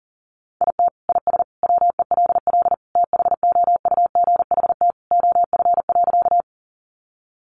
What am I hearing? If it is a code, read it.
Morse code, heard as a keyed tone, about 40 wpm, 700 Hz. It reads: IT IS WELL THOUGHT OF.
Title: Sentences from Top 300 Words 40wpm